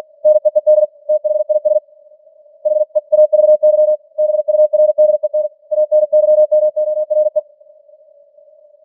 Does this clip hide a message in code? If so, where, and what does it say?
Yes: Morse code from 0.3 to 7.4 s, heard as a keyed tone, about 42 wpm, 610 Hz. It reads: NEED THIS HEAVY BURDEN ANYMORE